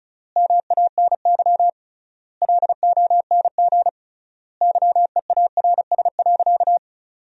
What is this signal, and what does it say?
Morse code, heard as a keyed tone, about 35 wpm, 700 Hz. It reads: MANY LONG YEARS.
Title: 3x Word-Spacing - Sentences from Top 200 Words - 35wpm